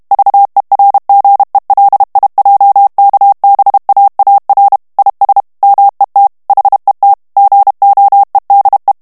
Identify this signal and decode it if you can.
Morse code, heard as a keyed tone, about 32 wpm, 800 Hz. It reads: VERGELIJKBAAR IS MET HET GOEDE
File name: HOLLANDAIS texte clair 2 vitesse 30 mots minute